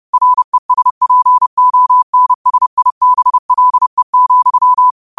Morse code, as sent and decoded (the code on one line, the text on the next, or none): .-. ... .--. --- -. ... .. -... .-.. --..--
RESPONSIBLE,